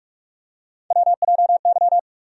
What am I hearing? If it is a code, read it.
Morse code, heard as a keyed tone, about 46 wpm, 700 Hz. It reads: WJY
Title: USA Call Sign Postfixes - 3 Letters 45wpm